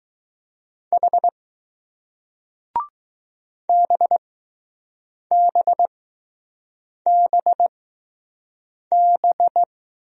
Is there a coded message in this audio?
Yes: Morse code from 0.9 to 9.6 s, heard as a keyed tone, about 20 wpm, spaced as for 6 wpm, 700 Hz. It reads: H BBBB